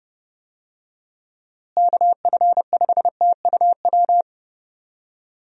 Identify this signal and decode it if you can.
Morse code, heard as a keyed tone, about 30 wpm, 700 Hz. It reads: KF5TUW
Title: USA Call Signs - 6 Characters - Repeated 3x - 30wpm